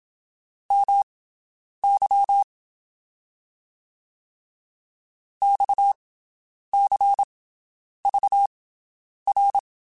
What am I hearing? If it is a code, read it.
Morse code, heard as a keyed tone, about 27 wpm, spaced as for 9 wpm, 800 Hz. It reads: MY XCVR
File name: QSOs Spaced